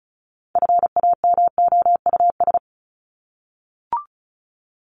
Morse code, heard as a keyed tone, about 34 wpm, 700 Hz. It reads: FAMOUS
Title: Top 1251-1500 Words 35wpm